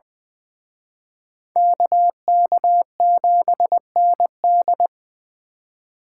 Morse code, heard as a keyed tone, about 24 wpm, 700 Hz. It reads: KK7ND
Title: Speed Racing - USA 2x2 Call Signs 20wpm